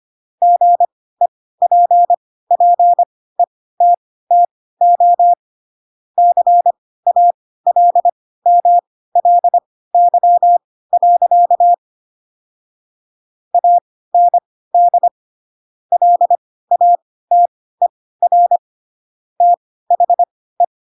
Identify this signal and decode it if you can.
Morse code, heard as a keyed tone, about 25 wpm, spaced as for 16 wpm, 700 Hz. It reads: GEPPETTO CALMLY. AND LATER THE
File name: Pinocchio / Pinocchio ch7 10 25 ik1hge